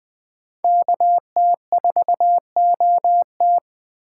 Morse code, heard as a keyed tone, about 20 wpm, 700 Hz. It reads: KT4OT
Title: USA Call Signs - 5 Characters - 3x Character Spacing - 20wpm